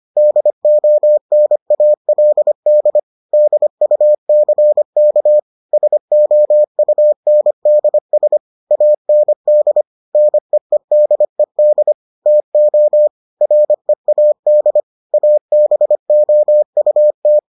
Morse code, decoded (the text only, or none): DONALD DUCK SOUNDS AND NEEDED TO READ ABOUT